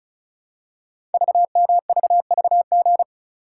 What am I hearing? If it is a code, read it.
Morse code, heard as a keyed tone, about 36 wpm, 690 Hz. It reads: VMVVG
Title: Code Groups - 5 Letters - 35wpm